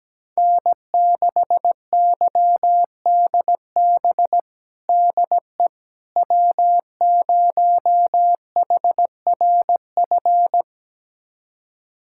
Morse code, heard as a keyed tone, about 17 wpm, 700 Hz. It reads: N6YDB DE W0HRF
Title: USA Callsign DE Callsign - 5 Characters - 17wpm